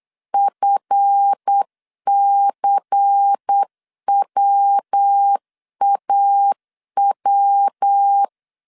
Morse code, heard as a keyed tone, about 8 wpm, 800 Hz. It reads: FCWAW
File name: LEZIONI / 7 GF9Z / 7 GF 8wpm 3108